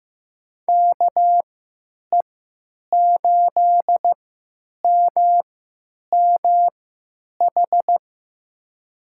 Morse code, heard as a keyed tone, about 15 wpm, spaced as for 9 wpm, 700 Hz. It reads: KE8MMH
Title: USA Call Signs - 6 Characters - 3x Character Spacing - 15wpm